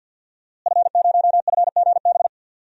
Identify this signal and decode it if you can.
Morse code, heard as a keyed tone, about 52 wpm, 700 Hz. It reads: F0FGB
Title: DXCC-100 Call Signs - 5 Characters - Repeated 3x - 50wpm